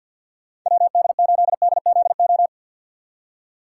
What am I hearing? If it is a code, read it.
Morse code, heard as a keyed tone, about 46 wpm, 700 Hz. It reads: WD8DZO
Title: USA Call Signs - 6 Characters - 3x Character Spacing - 50wpm